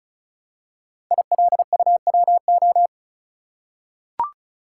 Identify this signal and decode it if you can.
Morse code, heard as a keyed tone, about 36 wpm, 700 Hz. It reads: ILUWO
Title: Code Groups - 5 Letters - 35wpm